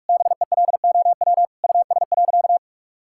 Morse code, heard as a keyed tone, about 41 wpm, 700 Hz. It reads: BELOW US.
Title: Sentences from Top 1000 Words - Review Part 5 - 45wpm